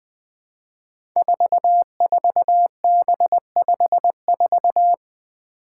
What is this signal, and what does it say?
Morse code, heard as a keyed tone, about 21 wpm, 700 Hz. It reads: 44B54
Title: Five Challenging Letter-Number Combinations 20wpm